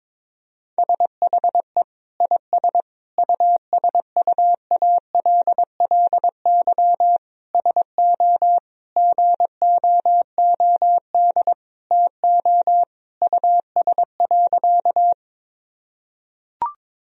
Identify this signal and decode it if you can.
Morse code, heard as a keyed tone, about 22 wpm, 700 Hz. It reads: SHE IS USUALLY SO GOOD TO US.
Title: Sentences from Top 1000 Words - Review Part 3 - 22wpm